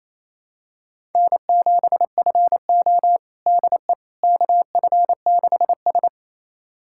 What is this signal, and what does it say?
Morse code, heard as a keyed tone, about 28 wpm, 700 Hz. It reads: N7FO DE KF6S
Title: USA Callsign DE Callsign - 4 Characters - 28wpm